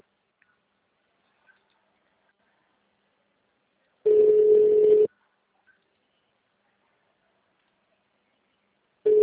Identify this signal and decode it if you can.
Ringback tone (Europe, 425 Hz).